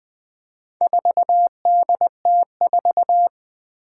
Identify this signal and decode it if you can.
Morse code, heard as a keyed tone, about 20 wpm, 700 Hz. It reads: 4DT4
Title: Four Challenging Letter-Number Combinations 20wpm